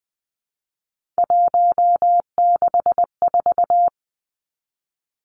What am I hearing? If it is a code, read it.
Morse code, heard as a keyed tone, about 20 wpm, 700 Hz. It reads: 164
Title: Three Challenging Letter-Number Combinations 20wpm